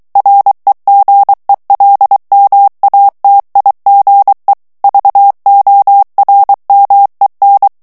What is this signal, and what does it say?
Morse code, heard as a keyed tone, about 23 wpm, 800 Hz. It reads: REGELMATIGE VORMEN
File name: HOLLANDAIS texte clair 1 vitesse 22 mots minute